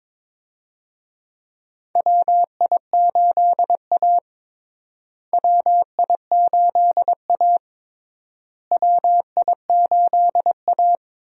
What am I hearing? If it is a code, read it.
Morse code, heard as a keyed tone, about 22 wpm, 700 Hz. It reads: WI8A WI8A WI8A
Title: USA Call Signs - 4 Characters - Repeated 3x - 22wpm